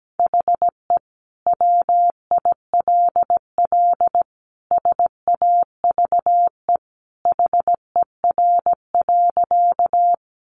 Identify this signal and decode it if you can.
Morse code, heard as a keyed tone, about 17 wpm, 700 Hz. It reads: HE WILL SAVE HER.
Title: Sentences from Top 1100 Words 17wpm